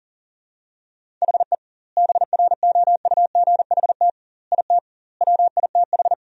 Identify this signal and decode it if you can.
Morse code, heard as a keyed tone, about 40 wpm, 700 Hz. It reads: HE BROUGHT IT WITH
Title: 2x Word-Spacing - Sentences from Top 500 Words - 40wpm